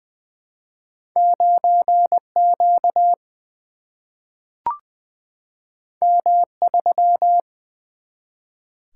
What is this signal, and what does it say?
Morse code, heard as a keyed tone, about 20 wpm, 700 Hz. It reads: 9Q M3
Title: Single-Letter Single-Number 20wpm